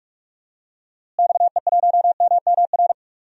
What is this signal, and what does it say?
Morse code, heard as a keyed tone, about 46 wpm, 700 Hz. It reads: XE1MMR
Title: DXCC-100 Call Signs - 6 Characters - Repeated 3x - 45wpm